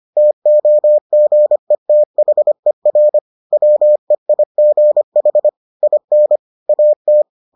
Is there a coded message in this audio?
Yes: Morse code from 0.2 to 7.2 s, heard as a keyed tone, about 25 wpm, 600 Hz. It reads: TOGETHER WEIGH IN AT